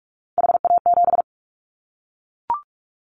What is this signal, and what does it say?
Morse code, heard as a keyed tone, about 44 wpm, 700 Hz. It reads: HA7
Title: Callsign Prefixes for Top 25 DXCC - 45wpm